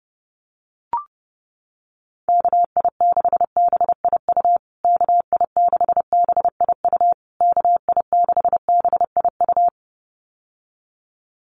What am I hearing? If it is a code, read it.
Morse code, heard as a keyed tone, about 30 wpm, 700 Hz. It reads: KI6BIU KI6BIU KI6BIU
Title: USA Call Signs Repeated 3x 30wpm